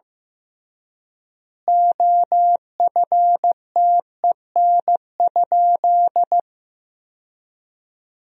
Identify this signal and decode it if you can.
Morse code, heard as a keyed tone, about 15 wpm, 700 Hz. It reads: OFTEN?
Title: 3x Word-Spacing - Sentences from Top 400 Words - 15wpm